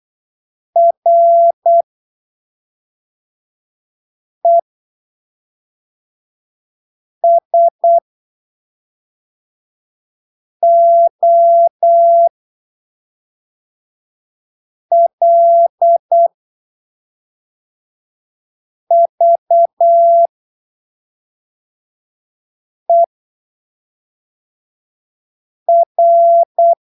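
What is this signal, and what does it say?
Morse code, heard as a keyed tone, about 8 wpm, spaced as for 3 wpm, 680 Hz. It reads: RESOLVER
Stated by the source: Morse training text at 8 ppm for Brazilian class B amateur radio telegraphy exams, from Portal do CW, PY2BRL.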